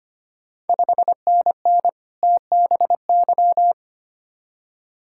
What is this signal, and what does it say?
Morse code, heard as a keyed tone, about 25 wpm, 700 Hz. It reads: 5NN TBY
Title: Ontario QSO Party Exchange 25wpm